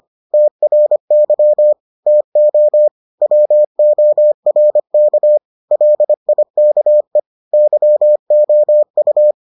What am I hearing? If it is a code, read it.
Morse code, heard as a keyed tone, about 25 wpm, 600 Hz. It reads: TRY TO WORK LIKE YOU